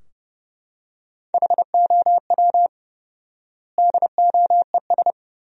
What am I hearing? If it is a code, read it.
Morse code, heard as a keyed tone, about 32 wpm, 700 Hz. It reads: HOW DOES
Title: 4x Word-Spacing - Sentences from Top 400 Words - 30wpm